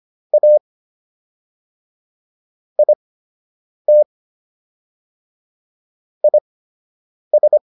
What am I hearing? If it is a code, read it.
Morse code, heard as a keyed tone, about 25 wpm, spaced as for 8 wpm, 600 Hz. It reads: A IT IS